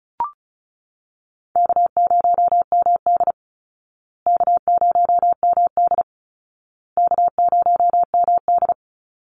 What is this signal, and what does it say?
Morse code, heard as a keyed tone, about 34 wpm, 700 Hz. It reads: K0MD K0MD K0MD
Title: Easy Call Signs Repeated 3x - 35wpm